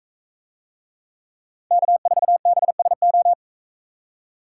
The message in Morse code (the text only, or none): K4BSO